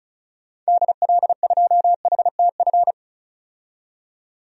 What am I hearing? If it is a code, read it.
Morse code, heard as a keyed tone, about 34 wpm, 700 Hz. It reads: DL2HTF
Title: DXCC-100 Call Signs - 6 Characters - 35wpm